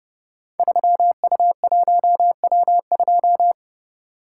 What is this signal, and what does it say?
Morse code, heard as a keyed tone, about 30 wpm, 700 Hz. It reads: 3U1W2
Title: Code Groups - 5 Letter:Numbers - 30wpm